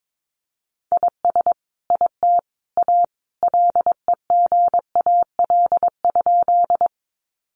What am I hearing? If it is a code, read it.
Morse code, heard as a keyed tone, about 22 wpm, 700 Hz. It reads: IS IT A LEGAL?
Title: Sentences from Top 1200 Words 22wpm